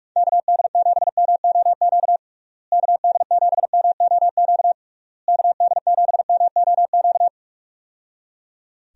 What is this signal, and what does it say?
Morse code, heard as a keyed tone, about 41 wpm, 700 Hz. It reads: KD7MOQ KD7MOQ KD7MOQ
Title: USA Call Signs - 6 Characters - Repeated 3x - 45wpm